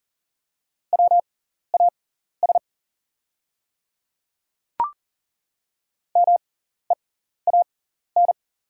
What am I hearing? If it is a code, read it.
Morse code, heard as a keyed tone, about 38 wpm, spaced as for 14 wpm, 700 Hz. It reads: WAS MEAN
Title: 6x Character Spacing - Top 200 Words - 40wpm